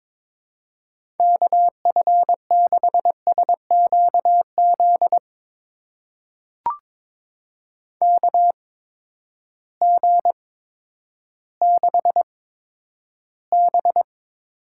Morse code, heard as a keyed tone, about 22 wpm, spaced as for 11 wpm, 700 Hz. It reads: KF6SQZ KG6B
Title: USA Call Signs - 6 Characters - 8x Character Spacing - 22wpm